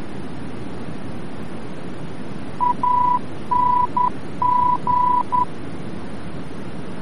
Morse code, heard as a keyed tone, about 10 wpm, 990 Hz, with noise. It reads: ANG
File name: france / mx / ANG-VOR-F